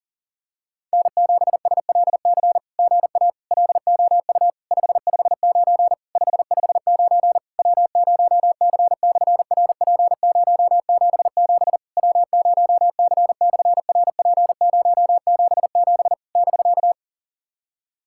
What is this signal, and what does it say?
Morse code, heard as a keyed tone, about 38 wpm, 700 Hz. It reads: N7SLC GA LOU 559 559 W0C/RP077 W0C/RP077 <BK>
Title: SOTA QSO Practice 40wpm